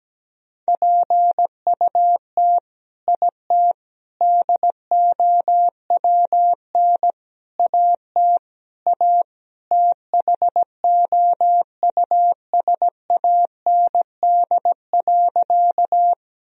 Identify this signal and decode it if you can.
Morse code, heard as a keyed tone, about 17 wpm, 700 Hz. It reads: PUT IT DOWN AT A THOUSAND.
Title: Sentences from Top 1000 Words - Review Part 3 - 17wpm